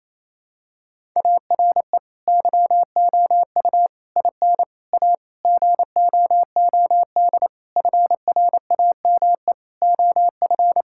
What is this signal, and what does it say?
Morse code, heard as a keyed tone, about 28 wpm, 700 Hz. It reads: ARE YOU IN A GOOD FRAME OF
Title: Sentences from Top 1000 Words - Review Part 6 - 28wpm